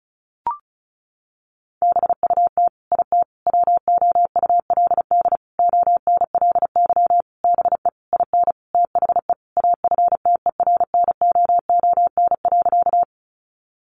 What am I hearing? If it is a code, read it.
Morse code, heard as a keyed tone, about 34 wpm, 700 Hz. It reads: BUT IT WOULD ONLY BE IN THE AFTERNOON.
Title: Sentences from Top 1500 Words 35wpm